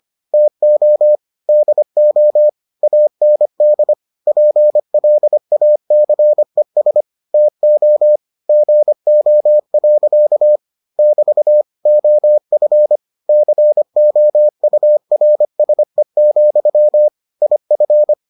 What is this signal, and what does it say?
Morse code, heard as a keyed tone, about 25 wpm, 600 Hz. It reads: TO DO AND PLACES TO GO. <BT> OF COURSE, IF